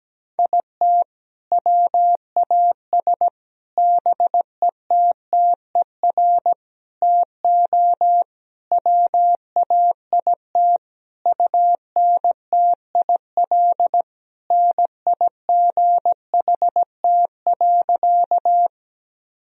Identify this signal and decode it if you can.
Morse code, heard as a keyed tone, about 17 wpm, 700 Hz. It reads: IT WAS BETTER TO WAIT UNTIL NIGHT.
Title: Sentences from Top 1000 Words - Review Part 1 - 17wpm